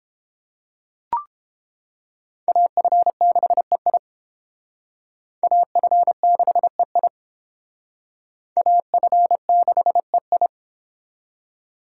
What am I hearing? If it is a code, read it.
Morse code, heard as a keyed tone, about 29 wpm, 700 Hz. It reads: AF6EI AF6EI AF6EI